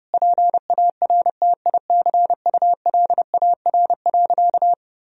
Morse code, heard as a keyed tone, about 32 wpm, 700 Hz. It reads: PARTICULAR.
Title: Sentences from Top 1000 Words - Review Part 4 - 30wpm